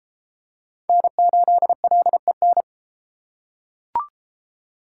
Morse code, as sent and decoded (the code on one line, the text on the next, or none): -. ---.. .-.. -.
N8LEN